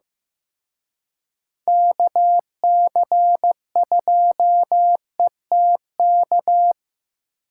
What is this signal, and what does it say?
Morse code, heard as a keyed tone, about 18 wpm, 700 Hz. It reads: KC2ETK
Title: Speed Racing - USA Call Signs 15wpm